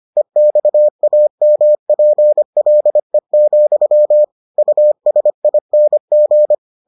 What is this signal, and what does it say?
Morse code, heard as a keyed tone, about 25 wpm, 600 Hz. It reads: EXAMPLE, USING